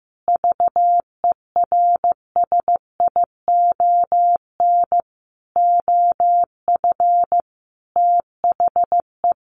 Morse code, heard as a keyed tone, about 15 wpm, 700 Hz. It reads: VERSION OF THE
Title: Sentences from Top 1400 Words 15wpm